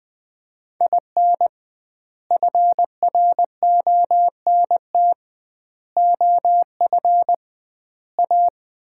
Morse code, heard as a keyed tone, about 20 wpm, 700 Hz. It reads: IN FRONT OF A
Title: 2x Word-Spacing - Sets of 4 Words - 20wpm